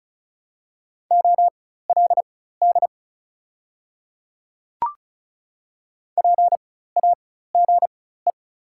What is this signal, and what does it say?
Morse code, heard as a keyed tone, about 36 wpm, spaced as for 16 wpm, 690 Hz. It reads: OLD PAGE
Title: 4x Character Spacing - Top 300 Words - 35wpm